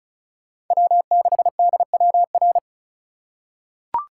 Morse code, heard as a keyed tone, about 34 wpm, 700 Hz. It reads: W6DWR
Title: USA Call Signs - 5 Characters - 2x Character Spacing - 35wpm